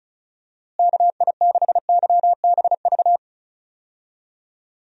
Morse code, heard as a keyed tone, about 34 wpm, 700 Hz. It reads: KI6YBV
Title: USA Call Signs - 6 Characters - Repeated 3x - 35wpm